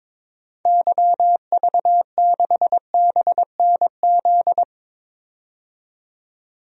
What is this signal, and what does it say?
Morse code, heard as a keyed tone, about 22 wpm, 700 Hz. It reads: YV6BNZ
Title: DXCC-100 Call Signs - 6 Characters - 22wpm